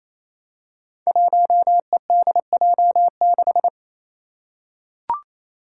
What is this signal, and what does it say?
Morse code, heard as a keyed tone, about 28 wpm, 700 Hz. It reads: 1EDJ6
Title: Five Challenging Letter-Number Combinations 28wpm